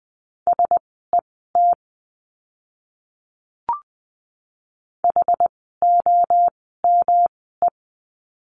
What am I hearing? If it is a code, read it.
Morse code, heard as a keyed tone, about 20 wpm, spaced as for 14 wpm, 700 Hz. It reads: SET HOME
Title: 108 Lesson 18f - Letter m - Words 20wpm